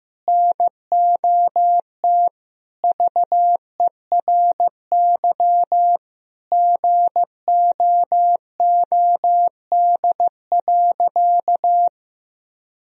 Morse code, heard as a keyed tone, about 15 wpm, 700 Hz. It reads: NOT VERY GOOD.
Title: Sentences from Top 1000 Words - Review Part 3 - 15wpm